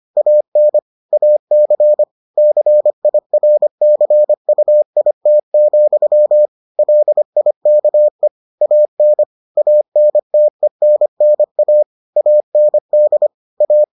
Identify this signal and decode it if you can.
Morse code, heard as a keyed tone, about 25 wpm, 600 Hz. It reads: AN AC CIRCUIT, LIKE AN ANTENNA AND A